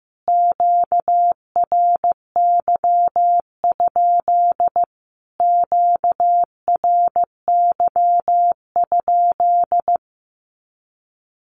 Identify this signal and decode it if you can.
Morse code, heard as a keyed tone, about 15 wpm, 700 Hz. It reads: QRY? QRY?
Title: Q-Codes Repeated 3x 15wpm